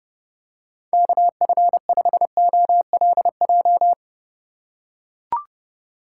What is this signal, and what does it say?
Morse code, heard as a keyed tone, about 30 wpm, 700 Hz. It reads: KF5OLJ
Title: USA Call Signs - 6 Characters - 2x Character Spacing - 30wpm